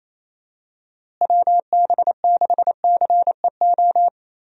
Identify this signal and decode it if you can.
Morse code, heard as a keyed tone, about 28 wpm, 700 Hz. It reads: WB6CEO
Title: USA Call Signs - 6 Characters - Repeated 3x - 28wpm